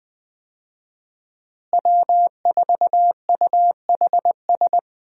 Morse code, heard as a keyed tone, about 20 wpm, 700 Hz. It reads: W4UHS
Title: USA Call Signs - 5 Characters - Repeated 2x - 20wpm